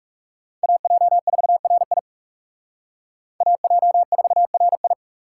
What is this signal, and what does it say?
Morse code, heard as a keyed tone, about 38 wpm, 700 Hz. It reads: AJ4RI AJ4RI